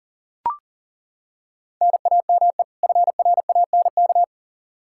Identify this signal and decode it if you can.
Morse code, heard as a keyed tone, about 40 wpm, 700 Hz. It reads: NAME FRANK
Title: Practice QSO - Exchange 1 - Beginning - 40wpm